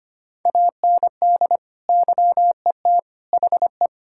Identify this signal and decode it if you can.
Morse code, heard as a keyed tone, about 25 wpm, 700 Hz. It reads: AND YET HE
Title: Sentences from Top 1600 Words 25wpm